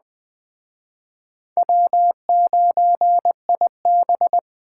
Morse code, heard as a keyed tone, about 21 wpm, 700 Hz. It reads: W9IB